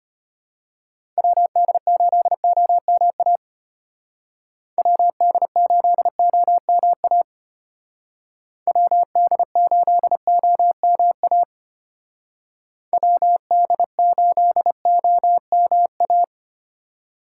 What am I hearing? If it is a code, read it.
Morse code, heard as a keyed tone, about 29 wpm, 700 Hz. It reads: WD8OMA WD8OMA WD8OMA WD8OMA
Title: Speed Racing - USA Call Signs 25wpm